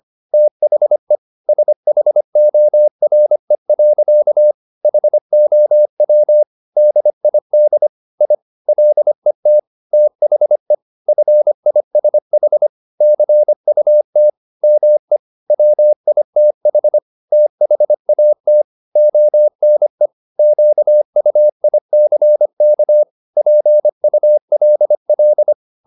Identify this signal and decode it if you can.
Morse code, heard as a keyed tone, about 25 wpm, 600 Hz. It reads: THE SHORE. HOW DID I LET THE FISH CUT ME WITH THAT ONE QUICK PULL
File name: the old man and the sea part 1 25wpm